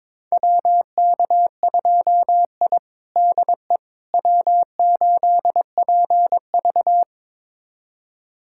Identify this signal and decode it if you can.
Morse code, heard as a keyed tone, about 22 wpm, 700 Hz. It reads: WK2I DE W8PV
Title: USA Callsign DE Callsign - 4 Characters - 22wpm